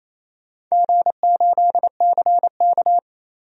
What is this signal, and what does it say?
Morse code, heard as a keyed tone, about 27 wpm, 700 Hz. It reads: G8CK
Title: DXCC-100 Call Signs - 4 Characters - 28wpm